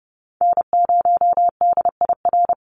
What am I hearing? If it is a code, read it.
Morse code, heard as a keyed tone, about 30 wpm, 700 Hz. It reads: N0DIR
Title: Call Signs Repeated 2x - 30wpm